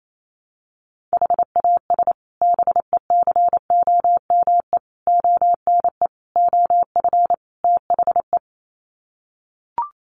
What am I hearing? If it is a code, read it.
Morse code, heard as a keyed tone, about 28 wpm, 700 Hz. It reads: HAS BECOME ONE OF THE
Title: Sets of 5 Words Encore 28wpm